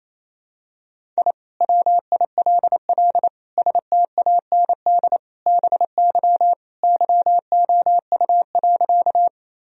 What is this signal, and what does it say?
Morse code, heard as a keyed tone, about 28 wpm, 700 Hz. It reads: I WILL STAND BY YOU.
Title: Sentences from Top 1000 Words - Review Part 1 - 28wpm